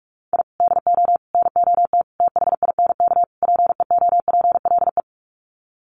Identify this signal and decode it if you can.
Morse code, heard as a keyed tone, about 44 wpm, 700 Hz. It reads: I DO NOT THINK PEOPLE
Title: Sets of 5 Words Encore2 45wpm